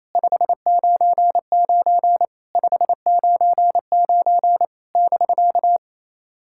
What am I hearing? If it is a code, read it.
Morse code, heard as a keyed tone, about 28 wpm, 700 Hz. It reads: 599 599 <BK>
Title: SOTA QSO Practice 28wpm